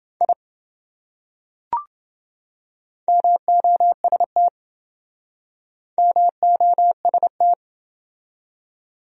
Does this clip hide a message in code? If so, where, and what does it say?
Yes: Morse code from 0.2 to 7.5 s, heard as a keyed tone, about 26 wpm, 700 Hz. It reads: I MOST MOST